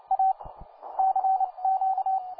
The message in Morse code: A CQ

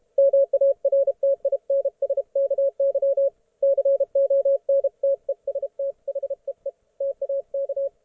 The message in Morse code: MARTINSKY CONTEST HEE TAK

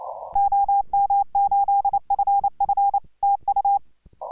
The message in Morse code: OM8FF TU

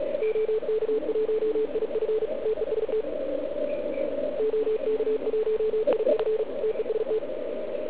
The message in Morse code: OK1IF TEST OK1IF TEST

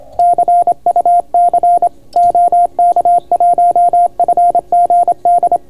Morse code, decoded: CUC OK1FGD

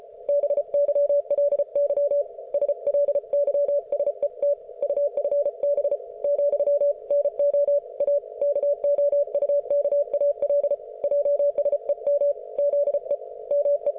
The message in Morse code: BYLY SLYSET UFB , NO A KOUKAL JSEM ZE ME

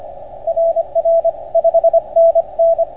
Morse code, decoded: RR 5NN

